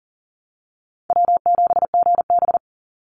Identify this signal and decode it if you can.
Morse code, heard as a keyed tone, about 40 wpm, 700 Hz. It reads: W7GB
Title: Easy Call Signs - 40wpm